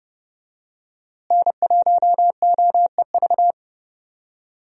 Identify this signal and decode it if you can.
Morse code, heard as a keyed tone, about 30 wpm, 700 Hz. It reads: N1OEV
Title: USA Call Signs - 5 Characters - Repeated 3x - 30wpm